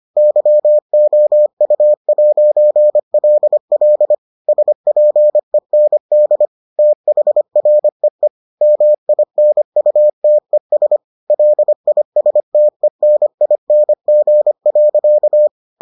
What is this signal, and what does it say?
Morse code, heard as a keyed tone, about 25 wpm, 600 Hz. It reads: YOU'LL SPEND THREE MINUTES LISTENING.